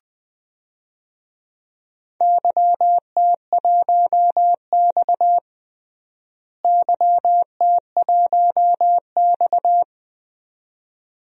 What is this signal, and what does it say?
Morse code, heard as a keyed tone, about 20 wpm, 700 Hz. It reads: YT1X YT1X
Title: DXCC-100 Call Signs - 4 Characters - Repeated 2x - 20wpm